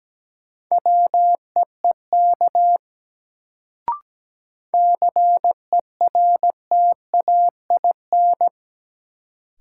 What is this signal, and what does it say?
Morse code, heard as a keyed tone, about 17 wpm, 700 Hz. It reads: WEEK CERTAIN
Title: Sentences from Top 500 Words - New Word Drill - 17wpm